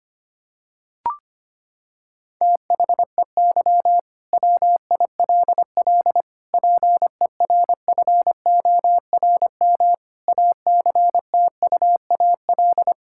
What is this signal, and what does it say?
Morse code, heard as a keyed tone, about 25 wpm, 700 Hz. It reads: THEY WILL PERFORM ACTUAL
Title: Sentences from Top 1000 Words - Review Part 5 - 25wpm